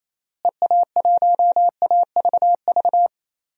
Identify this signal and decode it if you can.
Morse code, heard as a keyed tone, about 27 wpm, 700 Hz. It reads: EA1AVV
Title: DXCC-100 Call Signs - 6 Characters - Repeated 3x - 28wpm